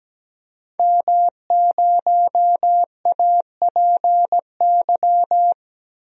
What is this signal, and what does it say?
Morse code, heard as a keyed tone, about 17 wpm, 700 Hz. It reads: M0APY
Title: DXCC-100 Call Signs - 5 Characters - Repeated 2x - 17wpm